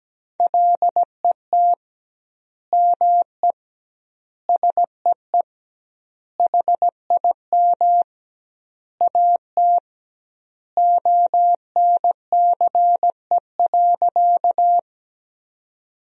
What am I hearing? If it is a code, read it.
Morse code, heard as a keyed tone, about 17 wpm, 700 Hz. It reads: LET ME SEE HIM AT ONCE.
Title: 2x Word-Spacing - Sentences from Top 300 Words - 17wpm